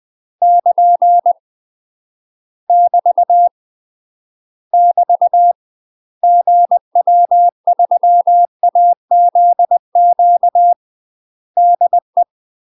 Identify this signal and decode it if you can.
Morse code, heard as a keyed tone, about 20 wpm, 700 Hz. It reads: <KN> <BT> <BT> GW3AZQ DE
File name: UK MASTER QSOs TRAINING/20WPM